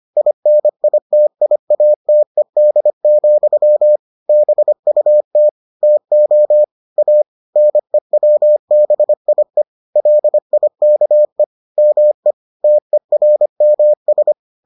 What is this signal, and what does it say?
Morse code, heard as a keyed tone, about 25 wpm, 600 Hz. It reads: INITIATED, BUT TO A NEWBIE LIKE ME TERMS